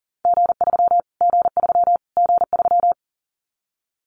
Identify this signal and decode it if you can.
Morse code, heard as a keyed tone, about 40 wpm, 700 Hz. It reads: G3 G3 G3
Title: Callsign Prefixes for Top 25 DXCC - Repeated 3x - 40wpm